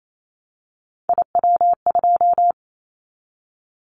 Morse code, heard as a keyed tone, about 28 wpm, 700 Hz. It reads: IW2
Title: Callsign Prefixes for Top 25 DXCC - 28wpm